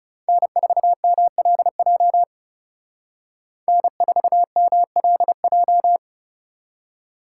Morse code, heard as a keyed tone, about 38 wpm, 700 Hz. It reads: N4MLJ N4MLJ